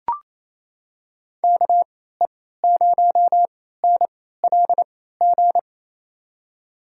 Morse code, heard as a keyed tone, about 28 wpm, spaced as for 16 wpm, 700 Hz. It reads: KE0NLG
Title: USA Call Signs - 6 Characters - 3x Character Spacing - 28wpm